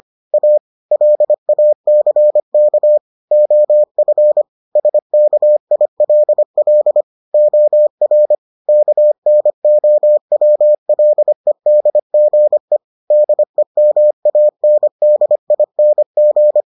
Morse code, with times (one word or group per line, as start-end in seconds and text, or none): 0.3-0.6 A
0.9-3.0 LACK
3.3-4.4 OF
4.8-7.0 SKILL
7.3-8.3 OR
8.7-12.8 KNOWLEDGE
13.1-16.6 DEMANDING